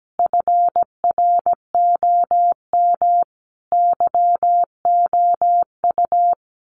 Morse code, heard as a keyed tone, about 17 wpm, 700 Hz. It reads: FROM YOU
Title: Common QSO Phrases 17wpm